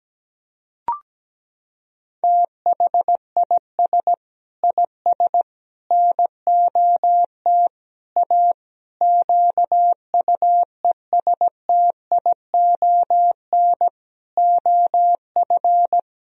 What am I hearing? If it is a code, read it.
Morse code, heard as a keyed tone, about 17 wpm, 700 Hz. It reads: THIS IS NOT A QUESTION OF
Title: Sentences from Top 1000 Words - Review Part 2 - 17wpm